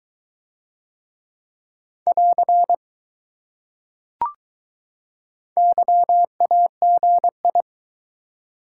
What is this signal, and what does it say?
Morse code, heard as a keyed tone, about 21 wpm, 700 Hz. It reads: <AR> YAGI